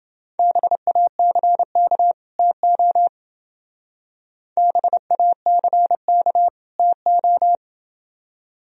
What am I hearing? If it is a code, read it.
Morse code, heard as a keyed tone, about 27 wpm, 700 Hz. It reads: BACK TO BACK TO